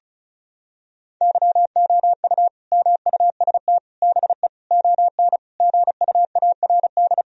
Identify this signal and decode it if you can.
Morse code, heard as a keyed tone, about 36 wpm, 700 Hz. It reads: YOU MUST BE ON GUARD
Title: Sentences from Top 1000 Words - Review Part 4 - 35wpm